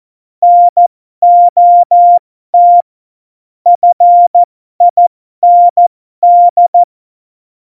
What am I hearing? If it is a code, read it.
Morse code, heard as a keyed tone, about 14 wpm, spaced as for 12 wpm, 700 Hz. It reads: NOT FIND
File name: Morsekurs / Witze